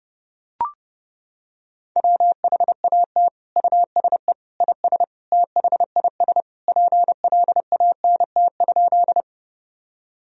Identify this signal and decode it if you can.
Morse code, heard as a keyed tone, about 30 wpm, 700 Hz. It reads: WHAT USE IS THIS PLANT?
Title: Sentences from Top 1000 Words - Review Part 1 - 30wpm